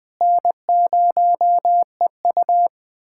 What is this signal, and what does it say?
Morse code, heard as a keyed tone, about 20 wpm, 700 Hz. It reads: N0EU
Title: USA Callsign DE Callsign - 4 Characters - 20wpm